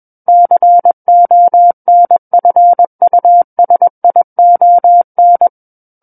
Morse code, heard as a keyed tone, about 21 wpm, 700 Hz. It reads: CONFUSION
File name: Book One / 12-Chapter 12 - 21 WPM